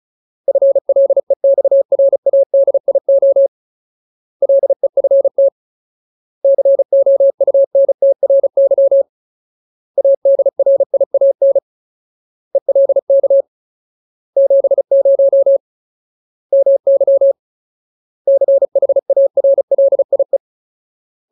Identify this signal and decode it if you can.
Morse code, heard as a keyed tone, about 34 wpm, 550 Hz. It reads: FLEXRADIO LEFT COUNTRY ADRIAN ELK 70 MY CHARLIE
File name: Ultimate Word List / 35WPM Ultimate